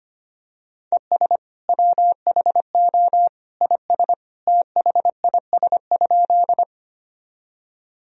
Morse code, heard as a keyed tone, about 25 wpm, 700 Hz. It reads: ES WHO IS THIS?
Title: Sentences from Top 100 Words and Abbreviations 25wpm